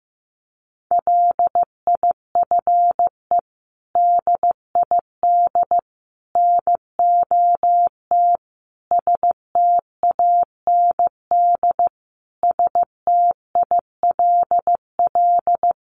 Morse code, heard as a keyed tone, about 15 wpm, 700 Hz. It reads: LIFE DID NOT STAND STILL
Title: Sentences from Top 800 Words 15wpm